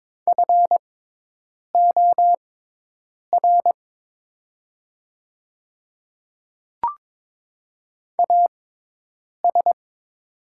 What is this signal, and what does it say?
Morse code, heard as a keyed tone, about 22 wpm, spaced as for 8 wpm, 700 Hz. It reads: FOR AS
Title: 6x Character Spacing - Top 100 Words - 22wpm